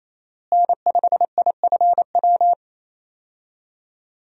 Morse code, heard as a keyed tone, about 28 wpm, 700 Hz. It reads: N5IFW
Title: USA Call Signs - 5 Characters - Repeated 2x - 28wpm